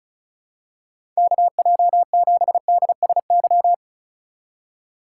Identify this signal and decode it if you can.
Morse code, heard as a keyed tone, about 36 wpm, 700 Hz. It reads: KJ7DSY
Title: USA Call Signs - 6 Characters - 4x Character Spacing - 35wpm